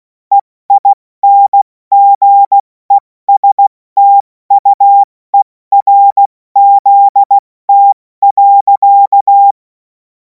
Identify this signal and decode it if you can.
Morse code, heard as a keyed tone, about 16 wpm, 800 Hz. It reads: EINGESTUERZT.
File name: Morsekurs / Witze / Witz 10. Februar 2026 70 F80